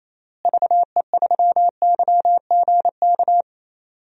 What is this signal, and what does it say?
Morse code, heard as a keyed tone, about 28 wpm, 700 Hz. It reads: VE3YGK
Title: DXCC-100 Call Signs - 6 Characters - Repeated 3x - 28wpm